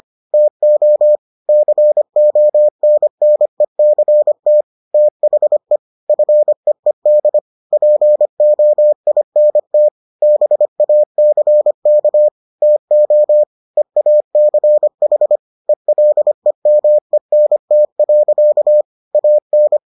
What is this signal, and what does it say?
Morse code, heard as a keyed tone, about 25 wpm, 600 Hz. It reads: TO CONNECT THE FEED POINT BACK TO EACH ELEMENT. AN